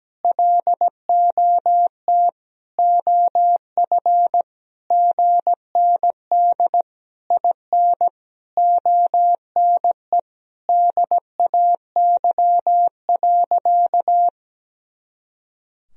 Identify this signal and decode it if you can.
Morse code, heard as a keyed tone, about 17 wpm, 700 Hz. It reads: LOT OF GND IN ONE DAY.
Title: Sentences from Top 500 Words and Abbreviations 17wpm